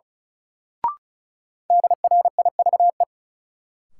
Morse code, heard as a keyed tone, about 36 wpm, 700 Hz. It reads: DRIVE
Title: Sentences from Top 500 Words - New Word Drill - 35wpm